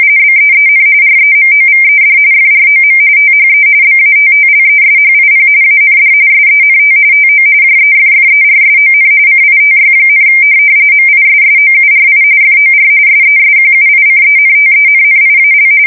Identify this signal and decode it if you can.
RTTY, about 45.45 baud. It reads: BEEN REMOVED FROM THE BATTLESHIP AND SENT TO JEFFERSON CITY, MISSOURI FOR SESQUICENTENNIAL
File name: fsk170 / USS-Missouri